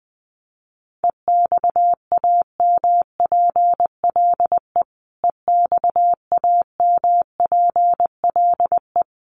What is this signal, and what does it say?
Morse code, heard as a keyed tone, about 20 wpm, 700 Hz. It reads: EXAMPLE EXAMPLE
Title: Top 500 Words Repeated 20wpm